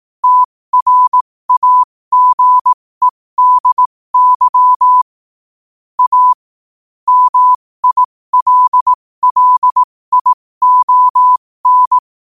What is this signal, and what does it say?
Morse code, heard as a keyed tone, about 18 wpm, spaced as for 15 wpm, 1000 Hz. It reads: TRAGEDY A MILLION
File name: qotdfiles / 15wpm